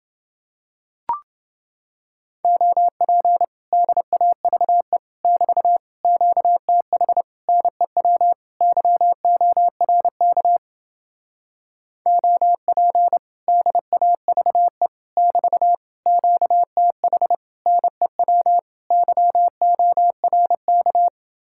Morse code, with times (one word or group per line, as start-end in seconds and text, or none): 2.4-3.4 OP
3.7-5.0 DAVE
5.2-5.8 <BT>
6.0-7.2 QTH
7.5-8.3 NEW
8.6-10.6 YORK
12.1-13.2 OP
13.5-14.9 DAVE
15.2-15.7 <BT>
16.1-17.3 QTH
17.7-18.6 NEW
18.9-21.1 YORK